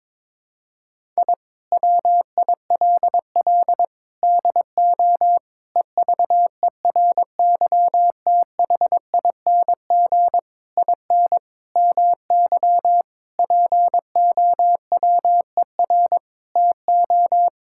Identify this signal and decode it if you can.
Morse code, heard as a keyed tone, about 22 wpm, 700 Hz. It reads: I WILL DO EVERYTHING IN MY POWER TO
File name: practice / 22wpm / Sentences with 8-Word Phrases - 22wpm